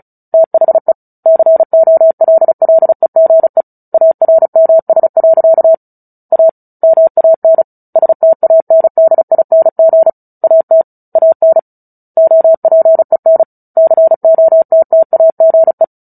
Morse code, heard as a keyed tone, about 34 wpm, 650 Hz. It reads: THE COLLEGE ARMS. A MAN STANDING AT AN OPEN COTTAGE
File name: Book Two / 09-Chapter 9 - 35 WPM